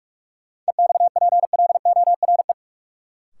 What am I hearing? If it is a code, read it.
Morse code, heard as a keyed tone, about 46 wpm, 700 Hz. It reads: EXPLORE
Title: Sentences from Top 1100 Words - New Word Drill Alt - 45wpm